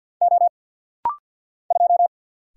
Morse code, heard as a keyed tone, about 49 wpm, 700 Hz. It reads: O 2
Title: Single Letter-Number Alt - Rapid-Fire 50wpm